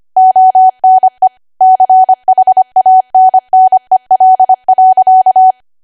Morse code, heard as a keyed tone, about 25 wpm, 750 Hz. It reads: ONE CHANNEL.
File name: Archive / 25 WPM / 25WPM